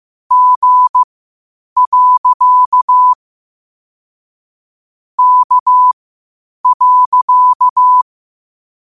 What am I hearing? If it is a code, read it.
Morse code, heard as a keyed tone, about 15 wpm, spaced as for 8 wpm, 1000 Hz. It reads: G. K.
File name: qotdfiles / 8wpm